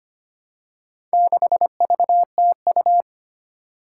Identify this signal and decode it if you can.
Morse code, heard as a keyed tone, about 25 wpm, 700 Hz. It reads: 6VTU
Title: Four Challenging Letter-Number Combinations 25wpm